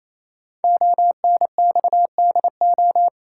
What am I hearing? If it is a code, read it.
Morse code, heard as a keyed tone, about 27 wpm, 700 Hz. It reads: ONXDO
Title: Code Groups - 5 Letters - 28wpm